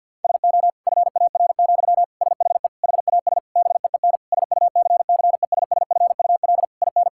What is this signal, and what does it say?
Morse code, heard as a keyed tone, about 52 wpm, 700 Hz. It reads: SO FAR, SHE HAS BEEN SUCCESSFUL IN
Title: Sentences from Top 1000 Words - Review Part 6 - 50wpm